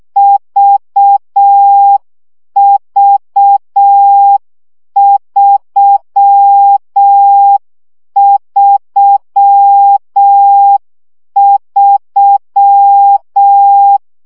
Morse code, heard as a keyed tone, about 6 wpm, 800 Hz. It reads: VV333
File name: CW gyakorlas v 3 5 baud